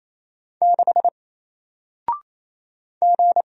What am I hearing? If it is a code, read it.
Morse code, heard as a keyed tone, about 27 wpm, 700 Hz. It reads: 6 G
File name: Single-Letter-Number-28wpm